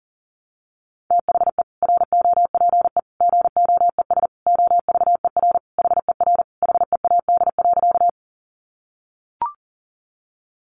Morse code, heard as a keyed tone, about 40 wpm, 700 Hz. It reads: THE ROPE GOES OVER HER HEAD.
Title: Easy Phrases 40wpm